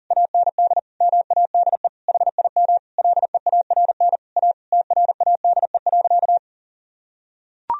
Morse code, heard as a keyed tone, about 40 wpm, 700 Hz. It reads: AND MADE HIM LEARN A TRADE.
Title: Sentences from Top 1000 Words - Review Part 4 - 40wpm